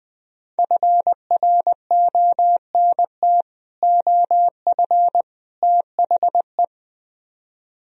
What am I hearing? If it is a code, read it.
Morse code, heard as a keyed tone, about 20 wpm, 700 Hz. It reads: FRONT OF THE